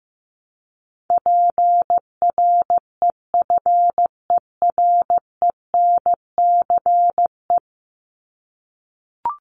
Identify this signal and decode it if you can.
Morse code, heard as a keyed tone, about 15 wpm, 700 Hz. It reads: PREFERENCE
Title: Top 1251-1500 Words 15wpm